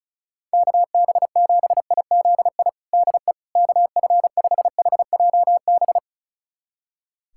Morse code, heard as a keyed tone, about 34 wpm, 700 Hz. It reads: KB7IZI DE KF5HJB
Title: USA Callsign DE Callsign - 6 Characters - 35wpm